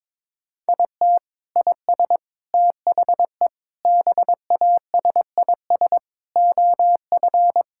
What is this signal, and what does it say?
Morse code, heard as a keyed tone, about 22 wpm, 700 Hz. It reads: IT IS THE BASIS OF
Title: Sentences from Top 1000 Words - Review Part 5 - 22wpm